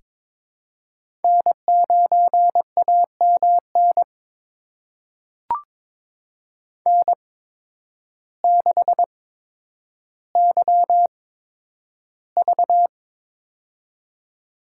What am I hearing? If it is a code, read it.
Morse code, heard as a keyed tone, about 22 wpm, spaced as for 10 wpm, 700 Hz. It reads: N9AMN N6YV
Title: USA Call Signs - 8x Character Spacing - 22wpm